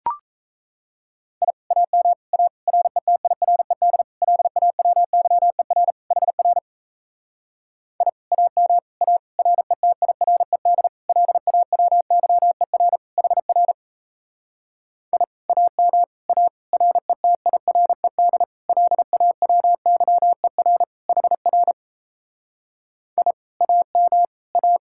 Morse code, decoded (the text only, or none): I AM A RETIRED LAWYER HR I AM A RETIRED LAWYER HR I AM A RETIRED LAWYER HR I AM A